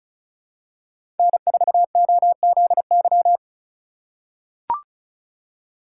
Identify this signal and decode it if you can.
Morse code, heard as a keyed tone, about 36 wpm, 700 Hz. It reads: N4OZY
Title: USA Call Signs - 5 Characters - 3x Character Spacing - 35wpm